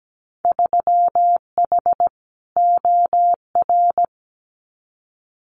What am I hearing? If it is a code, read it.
Morse code, heard as a keyed tone, about 17 wpm, 700 Hz. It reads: 3H OR
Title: Winter Field Day Exchanges Encore 17wpm